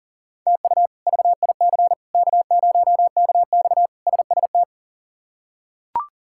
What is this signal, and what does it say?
Morse code, heard as a keyed tone, about 40 wpm, 700 Hz. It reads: TU VIC K0KX SST
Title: K1USN SST QSO Practice - 40wpm